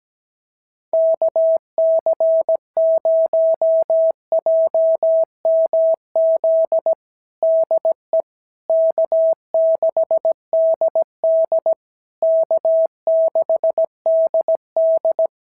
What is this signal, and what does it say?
Morse code, heard as a keyed tone, about 17 wpm, 650 Hz. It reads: KC0JMZ DE K6DD K6DD
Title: SOTA QSO Practice 17wpm